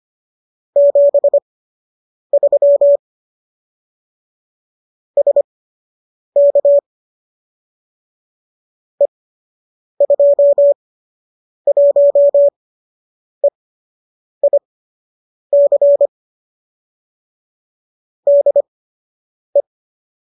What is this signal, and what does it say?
Morse code, heard as a keyed tone, about 25 wpm, spaced as for 8 wpm, 570 Hz. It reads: 73 SK E21EIC DE